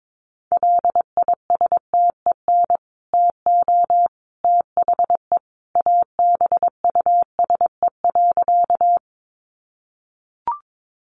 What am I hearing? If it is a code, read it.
Morse code, heard as a keyed tone, about 22 wpm, 700 Hz. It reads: LISTEN TO THE ABUSE.
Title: Sentences from Top 1300 Words 22wpm